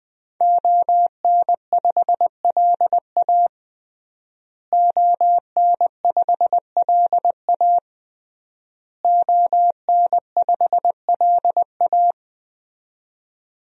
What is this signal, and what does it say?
Morse code, heard as a keyed tone, about 20 wpm, 700 Hz. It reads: ON5LA ON5LA ON5LA
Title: DXCC-100 Call Signs - 5 Characters - Repeated 3x - 20wpm